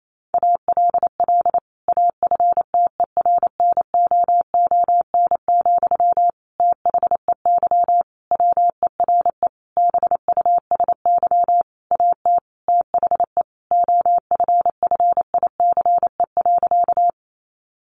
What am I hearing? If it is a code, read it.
Morse code, heard as a keyed tone, about 28 wpm, 700 Hz. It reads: ALL AFTERNOON, THEY WERE BUSY AT THE OFFICE.
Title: Sentences from Top 1500 Words 28wpm